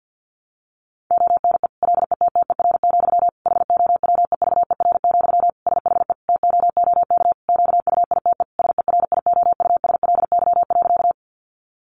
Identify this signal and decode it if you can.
Morse code, heard as a keyed tone, about 50 wpm, 700 Hz. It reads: ONE LETTER, HOWEVER, SHE TOOK QUITE SERIOUSLY.
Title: Sentences from Top 1200 Words 50wpm